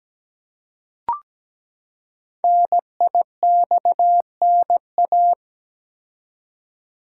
Morse code, heard as a keyed tone, about 17 wpm, 700 Hz. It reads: NIXNA
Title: Code Groups - 5 Letters - 17wpm